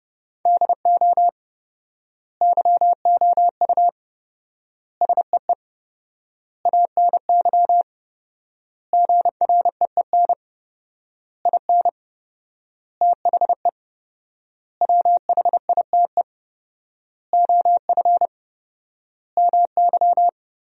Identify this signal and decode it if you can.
Morse code, heard as a keyed tone, about 30 wpm, 700 Hz. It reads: DO YOU SEE ANY GREEN IN THE WHITE OF MY
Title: 4x Word-Spacing - Sentences from Top 500 Words - 30wpm